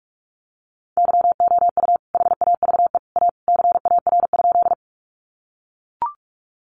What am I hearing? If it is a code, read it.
Morse code, heard as a keyed tone, about 44 wpm, 700 Hz. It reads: YOU HAVE A CAR?
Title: Sentences from Top 300 Words 45wpm